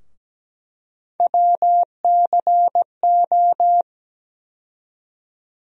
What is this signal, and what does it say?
Morse code, heard as a keyed tone, about 17 wpm, 700 Hz. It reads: WCO
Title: USA Call Sign Postfixes - 3 Letters 17wpm